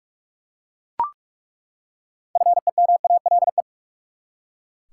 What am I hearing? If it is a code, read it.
Morse code, heard as a keyed tone, about 41 wpm, 700 Hz. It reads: FEMALE
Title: Six Letter Words - 45wpm